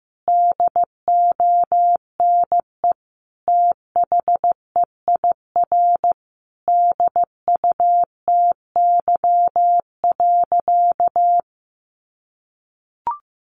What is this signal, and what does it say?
Morse code, heard as a keyed tone, about 15 wpm, 700 Hz. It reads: DONE THEIR DUTY.
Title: Sentences from Top 1500 Words 15wpm